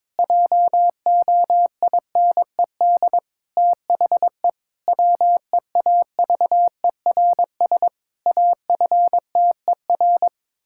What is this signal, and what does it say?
Morse code, heard as a keyed tone, about 22 wpm, 700 Hz. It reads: JOINED THE WEAVERS AFTER
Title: Pangram Challenge v2 - 22wpm